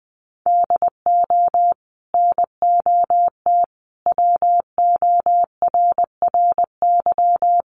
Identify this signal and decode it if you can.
Morse code, heard as a keyed tone, about 20 wpm, 700 Hz. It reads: DO NOT WORRY
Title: Sets of 5 Words 20wpm